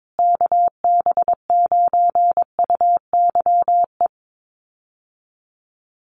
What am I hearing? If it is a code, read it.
Morse code, heard as a keyed tone, about 22 wpm, 700 Hz. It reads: KB9UYE
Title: USA Call Signs Repeated 3x 22wpm